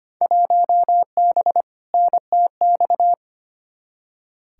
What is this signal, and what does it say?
Morse code, heard as a keyed tone, about 25 wpm, 700 Hz. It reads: 1B NTX
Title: Field Day Exchanges - 25wpm